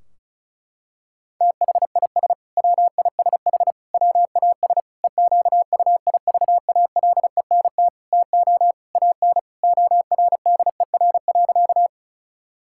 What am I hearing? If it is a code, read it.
Morse code, heard as a keyed tone, about 34 wpm, 700 Hz. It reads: THIS WISH WAS EQUIVALENT TO AN ORDER.
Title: Sentences from Top 1600 Words 35wpm